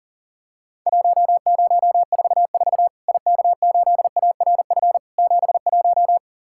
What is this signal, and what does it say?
Morse code, heard as a keyed tone, about 40 wpm, 700 Hz. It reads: 1044 IK8ARF 71
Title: Worked All Europe DX Contest - QTC Exchange - 40wpm